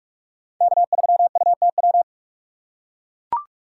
Morse code, heard as a keyed tone, about 44 wpm, 700 Hz. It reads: K3UTW